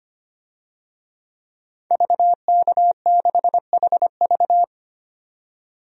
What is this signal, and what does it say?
Morse code, heard as a keyed tone, about 25 wpm, 700 Hz. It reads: VK6HV
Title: DXCC-100 Call Signs - 5 Characters - 25wpm